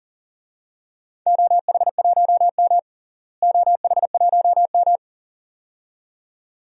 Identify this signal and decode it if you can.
Morse code, heard as a keyed tone, about 40 wpm, 690 Hz. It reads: OH1M OH1M
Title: DXCC-100 Call Signs - 4 Characters - Repeated 2x - 40wpm